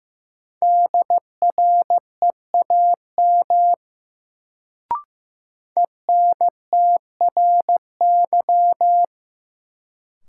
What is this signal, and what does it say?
Morse code, heard as a keyed tone, about 15 wpm, 700 Hz. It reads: DREAM ENTRY
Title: Sentences from Top 1500 Words - New Word Drill - 15wpm